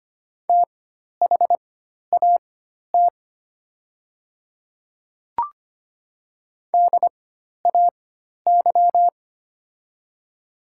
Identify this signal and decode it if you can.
Morse code, heard as a keyed tone, about 25 wpm, spaced as for 12 wpm, 700 Hz. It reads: THAT DAY
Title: 4x Character Spacing - Top 100 Words - 25wpm